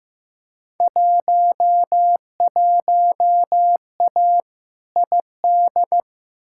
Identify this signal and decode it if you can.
Morse code, heard as a keyed tone, about 15 wpm, 700 Hz. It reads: 11A ID
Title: Field Day Exchanges Encore - 15wpm